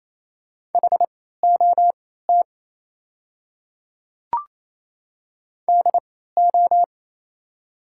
Morse code, heard as a keyed tone, about 28 wpm, spaced as for 16 wpm, 690 Hz. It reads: HOT DO
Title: 3x Character Spacing - Top 100 Words - 28wpm